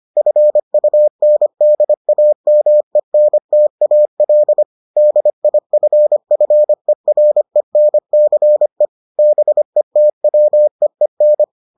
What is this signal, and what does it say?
Morse code, heard as a keyed tone, about 25 wpm, 600 Hz. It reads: FUNDAMENTAL DIFFERENCE BETWEEN